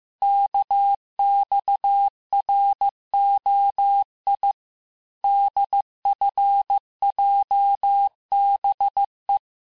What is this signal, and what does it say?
Morse code, heard as a keyed tone, about 15 wpm, 800 Hz. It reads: KXROI DFJBE
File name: cours-du-lundi files / cours-1585-texte-chiffre-vitesse-14mpm